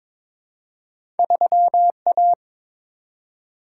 Morse code, heard as a keyed tone, about 22 wpm, 700 Hz. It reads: 3A
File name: practice / 22wpm / Two Letter-Number Combinations - 22wpm